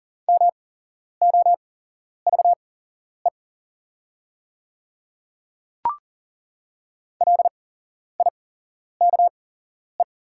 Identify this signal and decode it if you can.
Morse code, heard as a keyed tone, about 40 wpm, spaced as for 11 wpm, 690 Hz. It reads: MOVE LIKE
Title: 8x Character Spacing - Top 300 Words - 40wpm